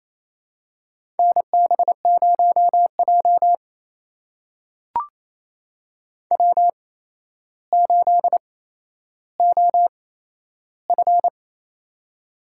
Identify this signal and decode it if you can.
Morse code, heard as a keyed tone, about 28 wpm, spaced as for 12 wpm, 700 Hz. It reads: NB0J W8OF
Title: USA Call Signs - 4 Characters - 8x Character Spacing - 28wpm